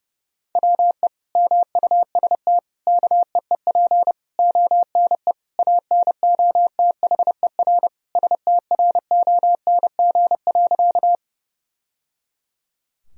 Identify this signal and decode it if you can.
Morse code, heard as a keyed tone, about 30 wpm, 700 Hz. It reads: WE MUST KEEP ONE ANOTHER STRONG.
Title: Sentences from Top 1000 Words - Review Part 3 - 30wpm